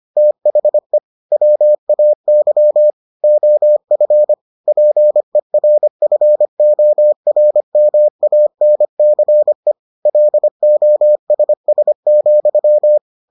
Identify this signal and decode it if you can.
Morse code, heard as a keyed tone, about 25 wpm, 600 Hz. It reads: THE WAY OF PERFORMANCE LOSS,